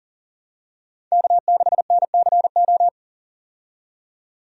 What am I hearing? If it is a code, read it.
Morse code, heard as a keyed tone, about 37 wpm, 700 Hz. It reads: K6NCO
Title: USA Call Signs - 5 Characters - 40wpm